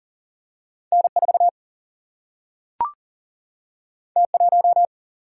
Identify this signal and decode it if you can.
Morse code, heard as a keyed tone, about 40 wpm, 700 Hz. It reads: N 4 T 1
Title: Single-Letter Single-Number 40wpm